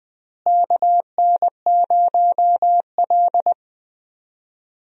Morse code, heard as a keyed tone, about 20 wpm, 700 Hz. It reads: KN0L
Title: USA Call Signs - 4 Characters - Repeated 2x - 20wpm